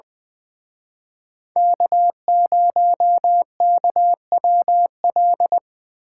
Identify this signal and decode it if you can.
Morse code, heard as a keyed tone, about 23 wpm, 700 Hz. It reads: K0KWL
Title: Speed Racing - USA 1x3 Call Signs 20wpm